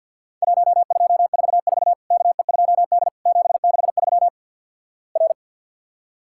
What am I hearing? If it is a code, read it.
Morse code, heard as a keyed tone, about 54 wpm, 700 Hz. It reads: 1244 KE2D 763 R
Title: Worked All Europe DX Contest - QTC Exchange with Repeats - 50wpm